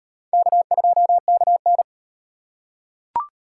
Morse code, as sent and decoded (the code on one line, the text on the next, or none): -.- ..--- -.- -.
K2KN